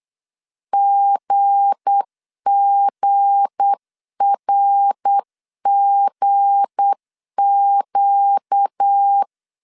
Morse code, heard as a keyed tone, about 8 wpm, 800 Hz. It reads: GGRGQ
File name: LEZIONI / 7 GF9Z / 7 G 8wpm 3118